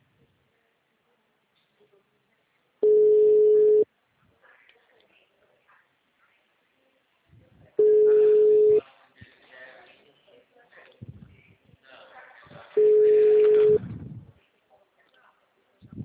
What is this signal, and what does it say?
Ringback tone (Europe, 425 Hz), with speech.
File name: out- -2016- - - .78